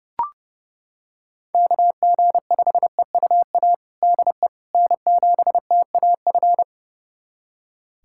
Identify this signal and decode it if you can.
Morse code, heard as a keyed tone, about 30 wpm, 700 Hz. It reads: KG5EUA DE N7TAF
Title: USA Callsign DE Callsign 30wpm